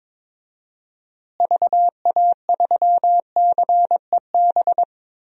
Morse code, heard as a keyed tone, about 22 wpm, 700 Hz. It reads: VA3CEB
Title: DXCC-100 Call Signs - 6 Characters - Repeated 2x - 22wpm